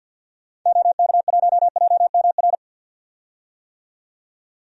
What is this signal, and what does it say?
Morse code, heard as a keyed tone, about 49 wpm, 700 Hz. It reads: OK1JMR
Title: DXCC-100 Call Signs - 6 Characters - 50wpm